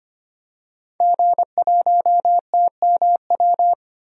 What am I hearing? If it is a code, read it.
Morse code, heard as a keyed tone, about 25 wpm, 700 Hz. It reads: G1TMW
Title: DXCC-100 Call Signs - 5 Characters - 25wpm